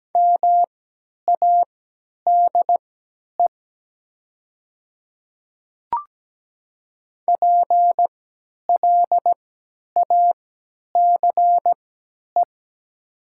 Morse code, heard as a keyed tone, about 17 wpm, spaced as for 10 wpm, 690 Hz. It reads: MADE PLACE
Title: 3x Character Spacing - Top 200 Words - 17wpm